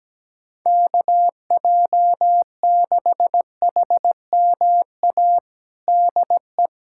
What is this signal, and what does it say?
Morse code, heard as a keyed tone, about 17 wpm, 700 Hz. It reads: KJ6HMA DE
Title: USA Callsign DE Callsign - 6 Characters - 17wpm